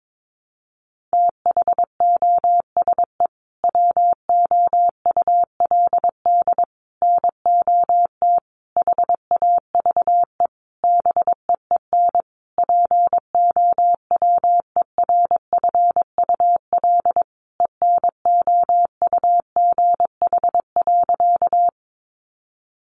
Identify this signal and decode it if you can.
Morse code, heard as a keyed tone, about 22 wpm, 700 Hz. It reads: THOSE WOULD NOT HAVE BEEN POWERFUL ENOUGH.
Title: Sentences from Top 1300 Words 22wpm